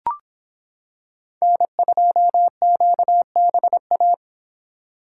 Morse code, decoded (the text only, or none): N2QBA